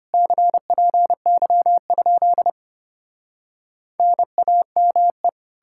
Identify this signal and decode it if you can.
Morse code, heard as a keyed tone, about 33 wpm, 700 Hz. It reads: CPY? NAME